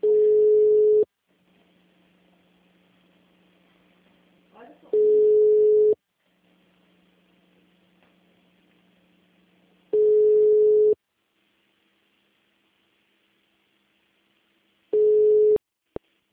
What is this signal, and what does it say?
Ringback tone (Europe, 425 Hz), with speech.
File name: out- -2011- - - .41513